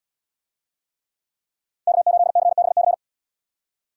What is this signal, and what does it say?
Morse code, heard as a keyed tone, about 78 wpm, 700 Hz. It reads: K8CBC